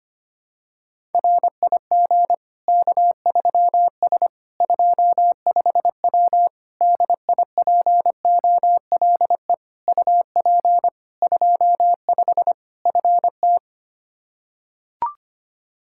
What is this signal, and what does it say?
Morse code, heard as a keyed tone, about 25 wpm, 700 Hz. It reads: RIG K3S 25W DIPOLE UP 25 FT
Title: K1USN SST QSO Practice 25wpm